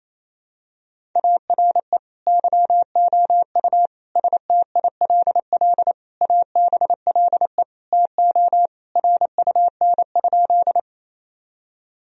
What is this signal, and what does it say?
Morse code, heard as a keyed tone, about 27 wpm, 700 Hz. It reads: ARE YOU STILL ABLE TO RUN?
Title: Sentences from Top 1000 Words - Review Part 2 - 28wpm